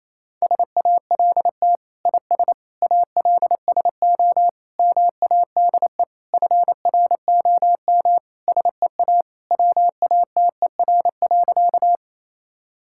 Morse code, heard as a keyed tone, about 28 wpm, 700 Hz. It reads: SALT IS ALSO MADE FROM SEA WATER.
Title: Sentences from Top 1000 Words - Review Part 5 - 28wpm